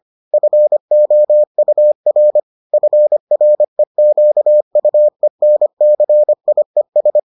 Morse code, heard as a keyed tone, about 25 wpm, 600 Hz. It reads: FOUR FREQUENCIES